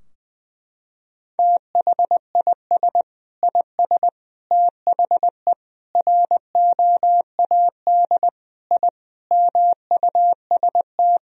Morse code, heard as a keyed tone, about 21 wpm, 700 Hz. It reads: THIS IS THE ROAD I MUST
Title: Sentences from Top 1000 Words - Review Part 2 - 20wpm